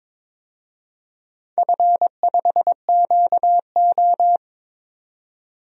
Morse code, heard as a keyed tone, about 22 wpm, 700 Hz. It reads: F5QO
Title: DXCC-100 Call Signs - 4 Characters - 22wpm